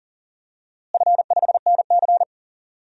Morse code, heard as a keyed tone, about 40 wpm, 700 Hz. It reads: F5NC
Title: DXCC-100 Call Signs - 4 Characters - Repeated 3x - 40wpm